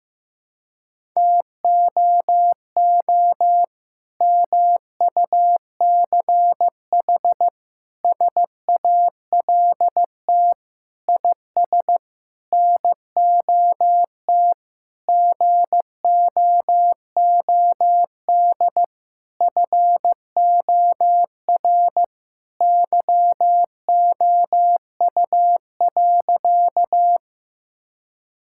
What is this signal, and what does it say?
Morse code, heard as a keyed tone, about 15 wpm, 700 Hz. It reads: TOO MUCH SALT IS NOT GOOD FOR YOU.
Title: Sentences from Top 1000 Words - Review Part 5 - 15wpm